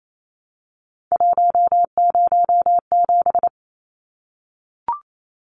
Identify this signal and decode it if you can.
Morse code, heard as a keyed tone, about 28 wpm, 700 Hz. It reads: 107
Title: Numbers - 3-Digit 28wpm